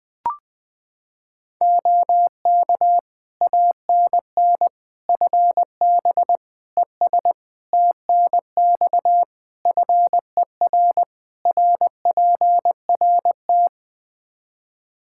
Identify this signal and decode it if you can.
Morse code, heard as a keyed tone, about 20 wpm, 700 Hz. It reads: OK ANN FB ES TNX FER RPRT
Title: K1USN SST QSO Practice 20wpm